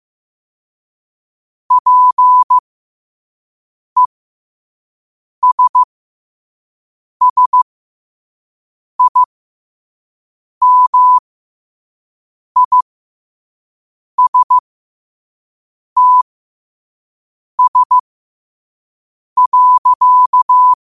Morse code, heard as a keyed tone, about 15 wpm, spaced as for 5 wpm, 1000 Hz. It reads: PESSIMISTS.
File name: qotdfiles / 5wpm